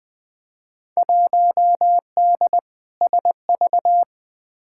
Morse code, heard as a keyed tone, about 20 wpm, 700 Hz. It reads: 1D SV
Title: Field Day Exchanges - 20wpm